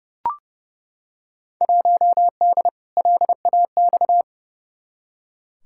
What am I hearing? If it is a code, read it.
Morse code, heard as a keyed tone, about 30 wpm, 700 Hz. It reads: 1D LAX
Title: Field Day Exchanges - 30wpm